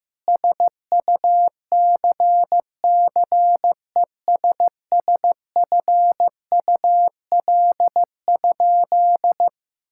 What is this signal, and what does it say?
Morse code, heard as a keyed tone, about 16 wpm, 700 Hz. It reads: SUCCESSFUL?
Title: Sentences from Top 1000 Words - Review Part 6 - 15wpm